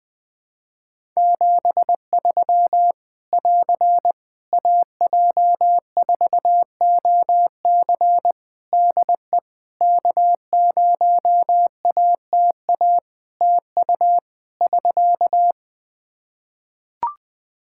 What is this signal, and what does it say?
Morse code, heard as a keyed tone, about 20 wpm, 700 Hz. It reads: 73 <AR> AJ4OC DE K0ATA TU <SK>
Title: Practice QSO - Exchange 4 - Ending - 20wpm